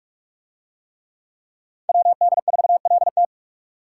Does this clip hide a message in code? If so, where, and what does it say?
Yes: Morse code from 1.9 to 3.3 s, heard as a keyed tone, about 45 wpm, 700 Hz. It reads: WD4LT